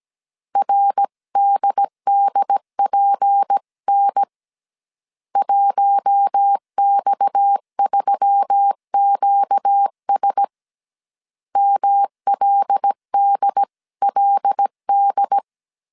Morse code, heard as a keyed tone, about 17 wpm, 800 Hz. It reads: RDDPN 1X3QS MLDLD
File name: CW / DET / 1 6 1500C S A 16wpm 2635